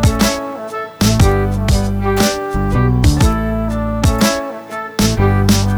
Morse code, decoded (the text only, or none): none